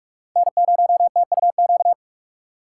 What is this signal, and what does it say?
Morse code, heard as a keyed tone, about 51 wpm, 700 Hz. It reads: N0TUQ
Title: Speed Racing - USA 1x3 Call Signs 45wpm